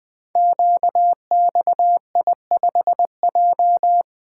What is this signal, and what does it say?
Morse code, heard as a keyed tone, about 20 wpm, 700 Hz. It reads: QXI5J
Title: Code Groups - 5 Letter:Numbers - 20wpm